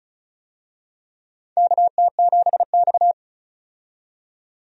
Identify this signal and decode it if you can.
Morse code, heard as a keyed tone, about 36 wpm, 700 Hz. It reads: KT7X
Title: USA Call Signs - 4 Characters - 35wpm